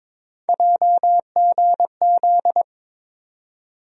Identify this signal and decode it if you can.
Morse code, heard as a keyed tone, about 22 wpm, 700 Hz. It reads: JGZ
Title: USA Call Sign Postfixes - 3 Letters 22wpm